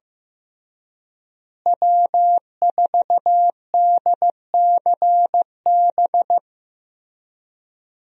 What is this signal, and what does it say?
Morse code, heard as a keyed tone, about 15 wpm, 700 Hz. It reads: W4DCB
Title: USA Call Signs - 5 Characters - Repeated 3x - 15wpm